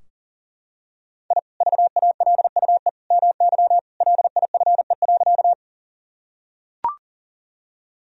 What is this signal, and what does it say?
Morse code, heard as a keyed tone, about 40 wpm, 700 Hz. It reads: I VALUE MY LIFE.
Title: Sentences from Top 1000 Words - Review Part 3 - 40wpm